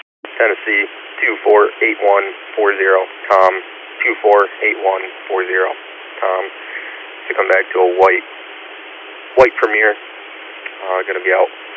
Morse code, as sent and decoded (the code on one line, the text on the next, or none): none